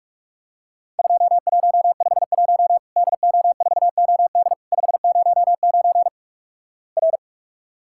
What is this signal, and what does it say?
Morse code, heard as a keyed tone, about 46 wpm, 700 Hz. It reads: 2151 DO4OD 509 R
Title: Worked All Europe DX Contest - QTC Exchange - 45wpm